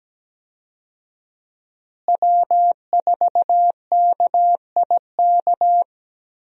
Morse code, heard as a keyed tone, about 17 wpm, 700 Hz. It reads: W4KIK
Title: USA Call Signs - 5 Characters - Repeated 2x - 17wpm